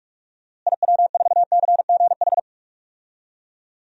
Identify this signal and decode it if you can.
Morse code, heard as a keyed tone, about 41 wpm, 700 Hz. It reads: IW4CGH
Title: DXCC-100 Call Signs - 6 Characters - Repeated 3x - 45wpm